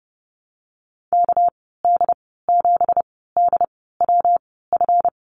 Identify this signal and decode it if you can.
Morse code, heard as a keyed tone, about 30 wpm, spaced as for 17 wpm, 700 Hz. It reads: KD7DWF
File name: Lesson 37g - Letter x - Call Signs 30wpm